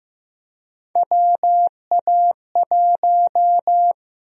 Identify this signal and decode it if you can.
Morse code, heard as a keyed tone, about 15 wpm, 700 Hz. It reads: WA1
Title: USA Call Sign Prefixes - 15wpm